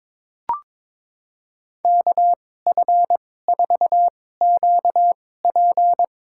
Morse code, heard as a keyed tone, about 22 wpm, spaced as for 16 wpm, 700 Hz. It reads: KF4QP
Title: USA Call Signs - 2x Character Spacing - 22wpm